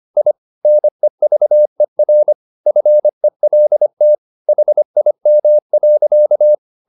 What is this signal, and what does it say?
Morse code, heard as a keyed tone, about 25 wpm, 600 Hz. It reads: I NEVER FELT HIM.
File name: the old man and the sea part 1 25wpm